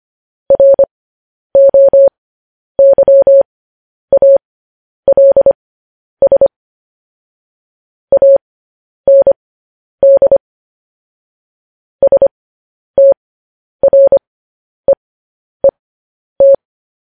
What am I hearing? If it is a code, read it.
Morse code, heard as a keyed tone, about 25 wpm, spaced as for 10 wpm, 550 Hz. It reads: ROYALS AND STREET
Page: news-headlines-in-morse-code-at-10-wpm